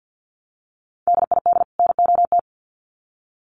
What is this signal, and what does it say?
Morse code, heard as a keyed tone, about 50 wpm, 700 Hz. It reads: DID NOT
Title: Sets of 2 Words 50wpm